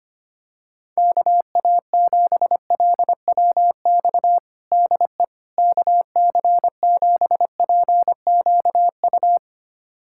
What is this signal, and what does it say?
Morse code, heard as a keyed tone, about 25 wpm, 700 Hz. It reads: KA7LWX DE KC7PQU
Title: USA Callsign DE Callsign - 6 Characters - 25wpm